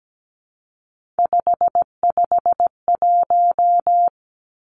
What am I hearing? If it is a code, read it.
Morse code, heard as a keyed tone, about 17 wpm, 700 Hz. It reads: 551
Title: Three Challenging Letter-Number Combinations 17wpm